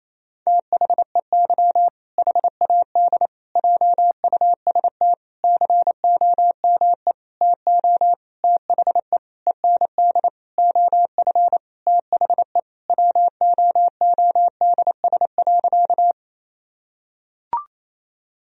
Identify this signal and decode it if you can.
Morse code, heard as a keyed tone, about 28 wpm, 700 Hz. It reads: THEY HAD JUST COME TO THE END OF THE WOODS.
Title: Sentences from Top 1000 Words - Review Part 2 - 28wpm